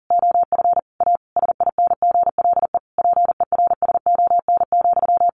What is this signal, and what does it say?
Morse code, heard as a keyed tone, about 40 wpm, 700 Hz. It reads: OF A SINGLE PERSON,
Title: Sentences from Top 1400 Words 40wpm